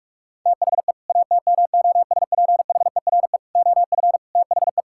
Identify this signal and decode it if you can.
Morse code, heard as a keyed tone, about 46 wpm, 700 Hz. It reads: THE ATMOSPHERE OF THE
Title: Sentences from Top 1600 Words 45wpm